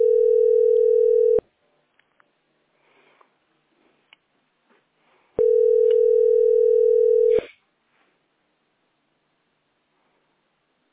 Ringback tone (North America), 440 + 480 Hz.